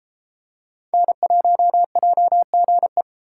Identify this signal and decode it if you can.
Morse code, heard as a keyed tone, about 31 wpm, 700 Hz. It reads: N1JGE